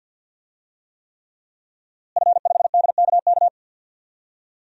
Morse code, heard as a keyed tone, about 51 wpm, 700 Hz. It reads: F5DKK
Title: DXCC-100 Call Signs - 5 Characters - Repeated 3x - 50wpm